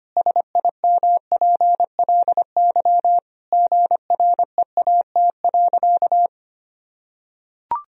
Morse code, heard as a keyed tone, about 25 wpm, 700 Hz. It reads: SIMPLY GREAT.
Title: Sentences from Top 1000 Words - Review Part 4 - 25wpm